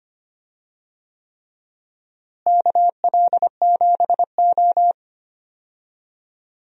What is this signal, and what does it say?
Morse code, heard as a keyed tone, about 25 wpm, 700 Hz. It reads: KL7O
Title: DXCC-100 Call Signs - 4 Characters - Repeated 3x - 25wpm